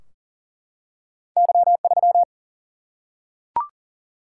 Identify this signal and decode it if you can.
Morse code, heard as a keyed tone, about 40 wpm, 700 Hz. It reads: Y3
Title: Single-Letter Single-Number 40wpm